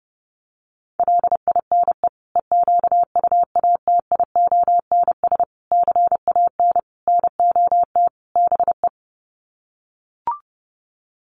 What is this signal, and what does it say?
Morse code, heard as a keyed tone, about 30 wpm, 700 Hz. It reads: LINE EQUATIONS CAN NOT BE
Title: Sets of 5 Words 30wpm